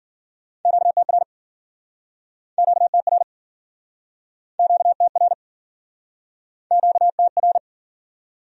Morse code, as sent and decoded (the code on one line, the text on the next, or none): --.- - .-. / --.- - .-. / --.- - .-. / --.- - .-.
QTR QTR QTR QTR